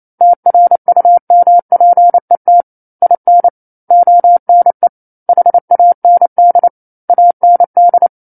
Morse code, heard as a keyed tone, about 28 wpm, 700 Hz. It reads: TRUMPET IN ONE HAND AND
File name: 09-Chapter-09 28 WPM